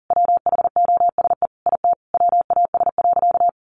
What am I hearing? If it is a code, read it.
Morse code, heard as a keyed tone, about 40 wpm, 700 Hz. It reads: WHOSE IT WAS.
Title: Sentences from Top 1200 Words 40wpm